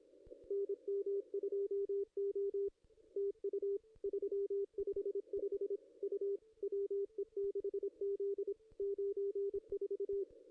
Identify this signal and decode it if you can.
Morse code, heard as a keyed tone, about 23 wpm, 410 Hz, with radio noise. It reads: NM2O TU 355 U WE6Z 94
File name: contests / ss-cw-k5tr-op-n5rz -0400